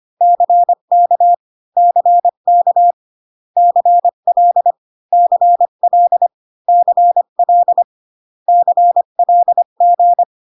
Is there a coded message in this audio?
Yes: Morse code from 0.2 to 10.2 s, heard as a keyed tone, about 25 wpm, 700 Hz. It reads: CK CK CL CL CL CLG